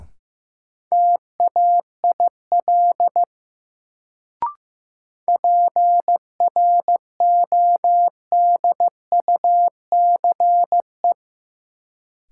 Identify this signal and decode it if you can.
Morse code, heard as a keyed tone, about 15 wpm, 700 Hz. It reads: TAIL PRODUCE
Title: Sentences from Top 1000 Words - Review Part 3 - New Word Drill - 15wpm